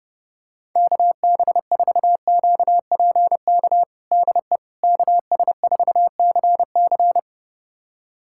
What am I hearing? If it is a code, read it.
Morse code, heard as a keyed tone, about 30 wpm, 700 Hz. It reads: KB4QPK DE KS4CC
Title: USA Callsign DE Callsign 30wpm